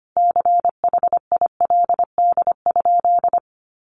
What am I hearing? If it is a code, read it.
Morse code, heard as a keyed tone, about 25 wpm, 700 Hz. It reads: CHILD?
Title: Sentences from Top 1300 Words 25wpm